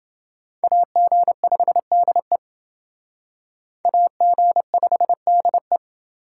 Morse code, heard as a keyed tone, about 28 wpm, 700 Hz. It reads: AG5DE AG5DE